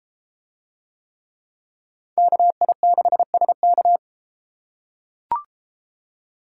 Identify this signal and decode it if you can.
Morse code, heard as a keyed tone, about 32 wpm, 700 Hz. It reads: KI6SK